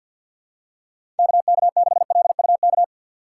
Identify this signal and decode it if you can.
Morse code, heard as a keyed tone, about 53 wpm, 700 Hz. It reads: KK6LUK
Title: USA Call Signs - 6 Characters - 8x Character Spacing - 50wpm